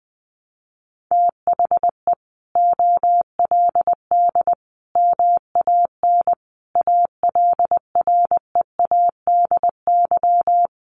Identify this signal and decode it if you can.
Morse code, heard as a keyed tone, about 20 wpm, 700 Hz. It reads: THE OLD MAN ALREADY
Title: Sentences from Top 700 Words 20wpm